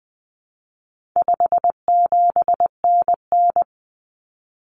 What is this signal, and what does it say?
Morse code, heard as a keyed tone, about 20 wpm, 700 Hz. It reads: 57NN
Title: Common Cut Numbers - 4-Digit 20wpm